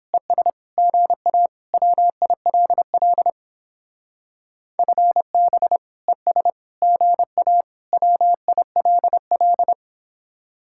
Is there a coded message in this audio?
Yes: Morse code from 0.1 to 9.7 s, heard as a keyed tone, about 31 wpm, 700 Hz. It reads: ES GA WILL FB ES GA WILL